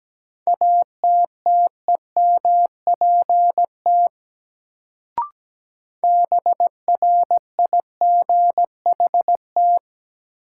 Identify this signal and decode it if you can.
Morse code, heard as a keyed tone, about 17 wpm, 700 Hz. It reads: ATTEMPT BRIGHT
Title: Sentences from Top 1000 Words - New Word Drill - 17wpm